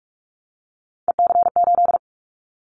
Morse code, heard as a keyed tone, about 44 wpm, 700 Hz. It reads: EC8
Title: Callsign Prefixes for Top 100 DXCC - 45wpm